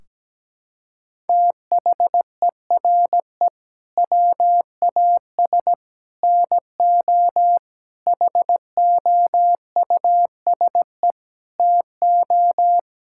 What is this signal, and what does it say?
Morse code, heard as a keyed tone, about 17 wpm, 700 Hz. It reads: THERE WAS NO HOUSE TO
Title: Sentences from Top 1600 Words 17wpm